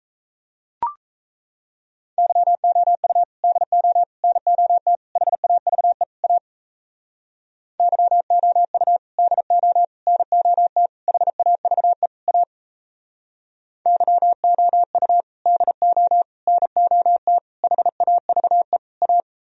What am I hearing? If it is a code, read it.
Morse code, heard as a keyed tone, about 37 wpm, 700 Hz. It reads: YOU DO NOT HAVE A YOU DO NOT HAVE A YOU DO NOT HAVE A